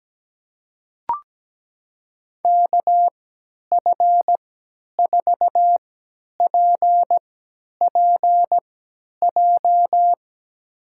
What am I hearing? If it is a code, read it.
Morse code, heard as a keyed tone, about 17 wpm, spaced as for 10 wpm, 700 Hz. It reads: KF4PPJ
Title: USA Call Signs - 6 Characters - 3x Character Spacing - 17wpm